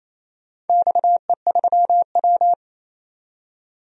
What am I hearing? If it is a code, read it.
Morse code, heard as a keyed tone, about 28 wpm, 700 Hz. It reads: XE3W
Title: DXCC-100 Call Signs - 4 Characters - 28wpm